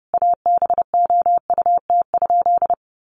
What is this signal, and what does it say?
Morse code, heard as a keyed tone, about 30 wpm, 700 Hz. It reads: ABOUT?
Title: Sentences from Top 100 Words 30wpm